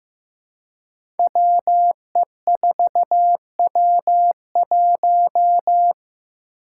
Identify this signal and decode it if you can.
Morse code, heard as a keyed tone, about 15 wpm, 700 Hz. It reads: WE4W1
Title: Five Challenging Letter-Number Combinations 15wpm